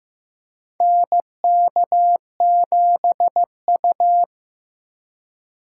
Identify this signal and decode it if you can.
Morse code, heard as a keyed tone, about 15 wpm, 700 Hz. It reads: NK7U
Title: K1USN SST QSO Practice - 15wpm